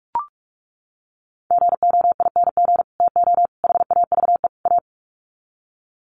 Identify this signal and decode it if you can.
Morse code, heard as a keyed tone, about 44 wpm, 700 Hz. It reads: GOING TO HAVE A
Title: Sets of 4 Words 45wpm